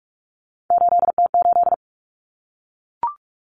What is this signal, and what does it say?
Morse code, heard as a keyed tone, about 44 wpm, 700 Hz. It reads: 8T8
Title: Common Cut Numbers - 3-Digit 45wpm